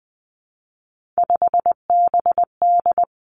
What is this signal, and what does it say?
Morse code, heard as a keyed tone, about 20 wpm, 700 Hz. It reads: 5BD
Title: Three Challenging Letter-Number Combinations 20wpm